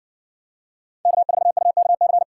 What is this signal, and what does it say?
Morse code, heard as a keyed tone, about 60 wpm, 700 Hz. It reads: K4UKC